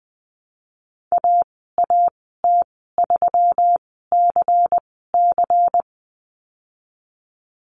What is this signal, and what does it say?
Morse code, heard as a keyed tone, about 19 wpm, spaced as for 14 wpm, 700 Hz. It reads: AAT3CC
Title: 128 Lesson 21g - Number 6 - Call Signs 20wpm